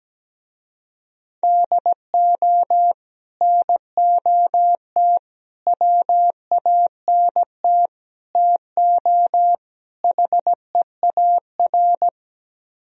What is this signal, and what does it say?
Morse code, heard as a keyed tone, about 17 wpm, 700 Hz. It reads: DO NOT WANT TO HEAR
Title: Speed-Racing - Sets of 5 Words 17wpm